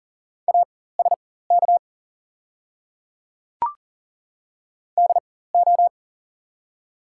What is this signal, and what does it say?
Morse code, heard as a keyed tone, about 40 wpm, spaced as for 19 wpm, 700 Hz. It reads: ASK DO
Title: 4x Character Spacing - Top 200 Words - 40wpm